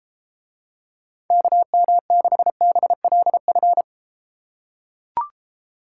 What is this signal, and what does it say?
Morse code, heard as a keyed tone, about 33 wpm, 700 Hz. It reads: KM6BLF